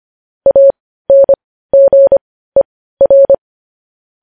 Morse code, heard as a keyed tone, about 25 wpm, spaced as for 15 wpm, 550 Hz. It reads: ANGER
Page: news headlines in morse code at 15 wpm